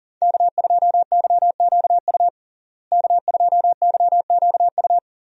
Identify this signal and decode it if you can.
Morse code, heard as a keyed tone, about 40 wpm, 700 Hz. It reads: K2YQU K2YQU
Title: USA Call Signs - 5 Characters - Repeated 3x - 40wpm